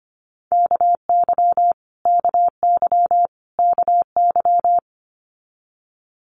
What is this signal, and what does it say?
Morse code, heard as a keyed tone, about 25 wpm, 700 Hz. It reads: KY KY KY
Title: Multi-letter Phonemes Repeated 3x 25wpm